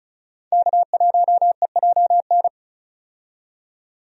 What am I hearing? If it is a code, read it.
Morse code, heard as a keyed tone, about 34 wpm, 700 Hz. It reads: K1EJN
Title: USA Call Signs - 5 Characters - 35wpm